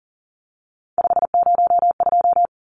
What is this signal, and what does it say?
Morse code, heard as a keyed tone, about 40 wpm, 700 Hz. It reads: 502
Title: Numbers - 3-Digit 40wpm